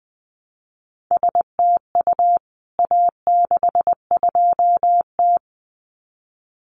Morse code, heard as a keyed tone, about 20 wpm, 700 Hz. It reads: STU A62T
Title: CWT Exchange 20wpm